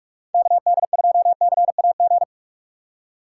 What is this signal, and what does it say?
Morse code, heard as a keyed tone, about 43 wpm, 700 Hz. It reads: KD2CAG
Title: USA Call Signs - 6 Characters - 3x Character Spacing - 45wpm